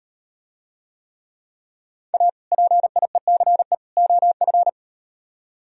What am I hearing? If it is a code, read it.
Morse code, heard as a keyed tone, about 38 wpm, 700 Hz. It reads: A PIECE OF